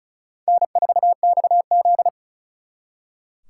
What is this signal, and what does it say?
Morse code, heard as a keyed tone, about 45 wpm, 700 Hz. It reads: N4XZ